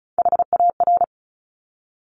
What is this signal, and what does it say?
Morse code, heard as a keyed tone, about 36 wpm, 700 Hz. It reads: HAR
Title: Top Three Letter Combinations in Words 35wpm